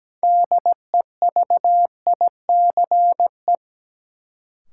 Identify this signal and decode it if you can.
Morse code, heard as a keyed tone, about 17 wpm, 700 Hz. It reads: DEVICE
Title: Sentences from Top 1500 Words - New Word Drill - 17wpm